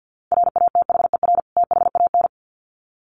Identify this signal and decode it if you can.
Morse code, heard as a keyed tone, about 50 wpm, 700 Hz. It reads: RATHER THAN
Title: Sets of 2 Words 50wpm